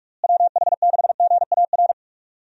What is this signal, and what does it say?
Morse code, heard as a keyed tone, about 44 wpm, 700 Hz. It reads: WH6GAR
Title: USA Call Signs - 6 Characters - Repeated 3x - 45wpm